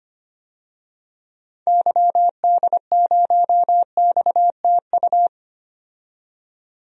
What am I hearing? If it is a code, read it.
Morse code, heard as a keyed tone, about 25 wpm, 700 Hz. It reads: YD0XTU
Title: DXCC-100 Call Signs - 6 Characters - 25wpm